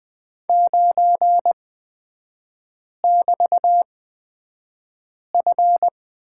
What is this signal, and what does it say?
Morse code, heard as a keyed tone, about 20 wpm, spaced as for 5 wpm, 700 Hz. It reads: 9<BT>F
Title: 224 Lesson 36e - BT - Three Characters 20wpm